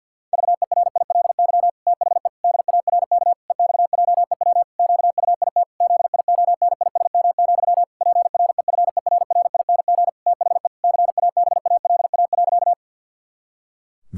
Morse code, heard as a keyed tone, about 51 wpm, 700 Hz. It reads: VERILY THE DARK EXJEW QUIT ZIONISM, PREFERRING THE CABALA.
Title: Pangram Challenge v2 - 50wpm